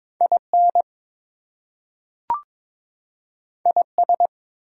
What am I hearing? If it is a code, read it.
Morse code, heard as a keyed tone, about 22 wpm, 700 Hz. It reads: IN IS
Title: Two Letter Words - 22wpm